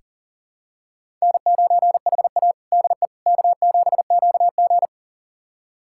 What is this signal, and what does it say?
Morse code, heard as a keyed tone, about 40 wpm, 700 Hz. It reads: N9HA DE K7QG
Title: USA Callsign DE Callsign - 4 Characters - 40wpm